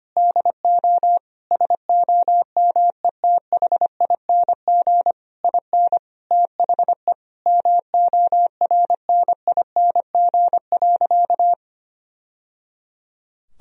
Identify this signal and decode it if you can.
Morse code, heard as a keyed tone, about 25 wpm, 700 Hz. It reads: DO SOMETHING IN THE MORNING.
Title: Sentences from Top 1000 Words - Review Part 4 - 25wpm